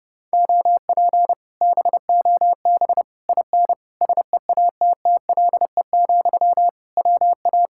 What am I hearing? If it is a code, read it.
Morse code, heard as a keyed tone, about 30 wpm, 700 Hz. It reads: OP BOB IN SEATTLE, WA